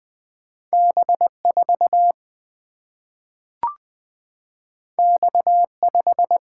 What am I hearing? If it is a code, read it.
Morse code, heard as a keyed tone, about 20 wpm, 700 Hz. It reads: B4 X5
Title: Single-Letter Single-Number 20wpm